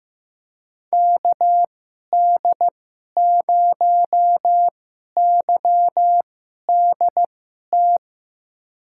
Morse code, heard as a keyed tone, about 15 wpm, spaced as for 11 wpm, 700 Hz. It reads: KD0YDT
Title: USA Call Signs - 6 Characters - 2x Character Spacing - 15wpm